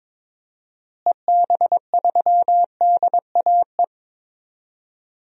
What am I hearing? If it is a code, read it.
Morse code, heard as a keyed tone, about 22 wpm, 700 Hz. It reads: EB3DAE
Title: DXCC-100 Call Signs - 6 Characters - Repeated 3x - 22wpm